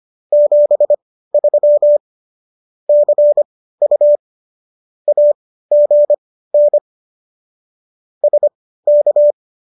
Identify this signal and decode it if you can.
Morse code, heard as a keyed tone, about 25 wpm, spaced as for 15 wpm, 590 Hz. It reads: 73 CU AGN SK